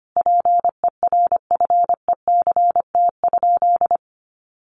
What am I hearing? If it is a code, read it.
Morse code, heard as a keyed tone, about 25 wpm, 700 Hz. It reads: PERFECT?
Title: Sentences from Top 1000 Words 25wpm